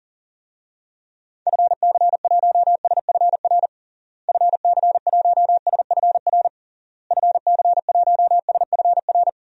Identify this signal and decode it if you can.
Morse code, heard as a keyed tone, about 40 wpm, 700 Hz. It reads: FC1SFR FC1SFR FC1SFR
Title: DXCC-100 Call Signs - 6 Characters - Repeated 3x - 40wpm